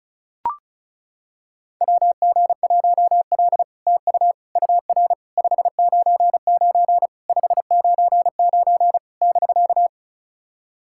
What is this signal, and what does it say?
Morse code, heard as a keyed tone, about 36 wpm, 700 Hz. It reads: WG1L TU UR 599 599 <BK>
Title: SOTA QSO Practice 35wpm